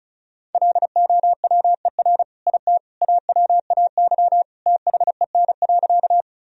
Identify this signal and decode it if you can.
Morse code, heard as a keyed tone, about 36 wpm, 700 Hz. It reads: LOWER IT AWAY THEN.
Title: Sentences from Top 1000 Words - Review Part 4 - 35wpm